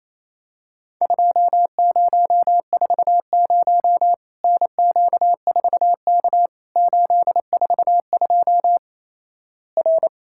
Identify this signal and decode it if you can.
Morse code, heard as a keyed tone, about 28 wpm, 700 Hz. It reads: 2040 NQ4K 842 R
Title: Worked All Europe DX Contest - QTC Exchange - 28wpm